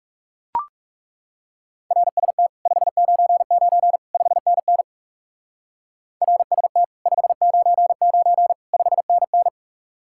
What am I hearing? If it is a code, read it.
Morse code, heard as a keyed tone, about 40 wpm, 700 Hz. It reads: RST 599 5NN RST 599 5NN